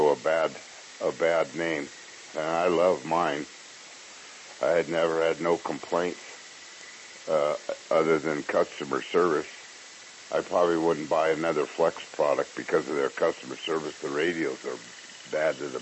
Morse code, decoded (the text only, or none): none